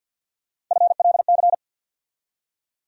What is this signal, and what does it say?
Morse code, heard as a keyed tone, about 48 wpm, 700 Hz. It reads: FLC
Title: USA Call Sign Postfixes - 3 Letters 50wpm